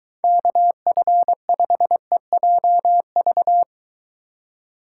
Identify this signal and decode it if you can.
Morse code, heard as a keyed tone, about 23 wpm, 700 Hz. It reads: KF5EJV